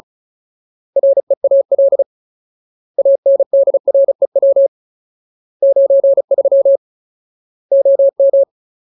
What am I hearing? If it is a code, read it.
Morse code, heard as a keyed tone, about 34 wpm, 550 Hz. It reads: REAL ANDREW 93 OM
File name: Ultimate Word List / 35WPM Ultimate